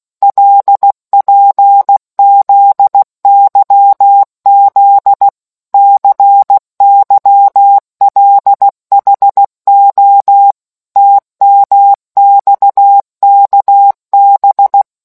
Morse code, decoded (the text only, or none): LPZYZ CYLHO TMXKB